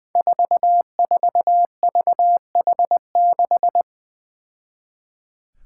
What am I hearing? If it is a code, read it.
Morse code, heard as a keyed tone, about 21 wpm, 700 Hz. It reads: 44VH6
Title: Five Challenging Letter-Number Combinations 20wpm